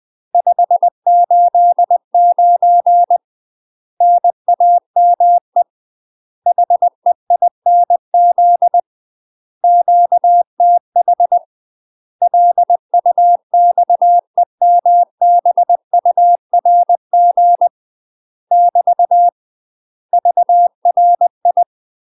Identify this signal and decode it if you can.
Morse code, heard as a keyed tone, about 20 wpm, 700 Hz. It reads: 589 NAME HEINZ QTH LUXEMBURG <BT> VRI
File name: UK MASTER QSOs TRAINING / 20WPM / UK MASTER QSOs 0036